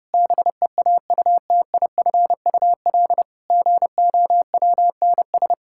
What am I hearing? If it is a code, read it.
Morse code, heard as a keyed tone, about 30 wpm, 700 Hz. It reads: BEAUTIFUL GOWNS
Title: Pangram Challenge v2 - 30wpm